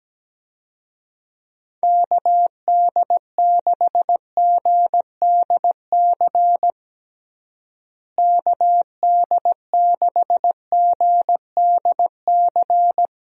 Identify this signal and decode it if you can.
Morse code, heard as a keyed tone, about 17 wpm, 700 Hz. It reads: KD6GDC KD6GDC
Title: USA Call Signs - 6 Characters - Repeated 2x - 17wpm